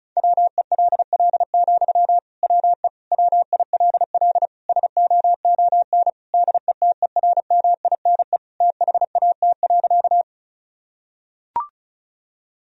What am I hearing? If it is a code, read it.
Morse code, heard as a keyed tone, about 34 wpm, 700 Hz. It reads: WELL, WE WILL SOON DETERMINE THAT.
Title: Sentences from Top 1000 Words - Review Part 4 - 35wpm